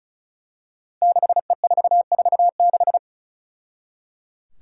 Morse code, heard as a keyed tone, about 34 wpm, 700 Hz. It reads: 6E446
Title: Five Challenging Letter-Number Combinations 35wpm